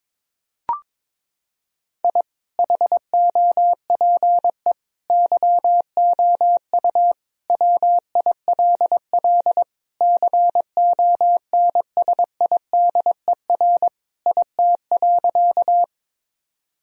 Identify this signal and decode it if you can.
Morse code, heard as a keyed tone, about 22 wpm, 700 Hz. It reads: I HOPE YOU WILL CONSIDER IT.
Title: Sentences from Top 1000 Words - Review Part 4 - 22wpm